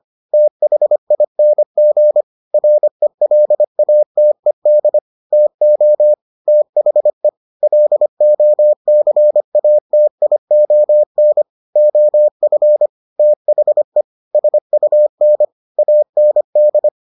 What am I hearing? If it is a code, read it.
Morse code, heard as a keyed tone, about 25 wpm, 600 Hz. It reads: THING RELATED TO THE LOCATION OF THE SUN AND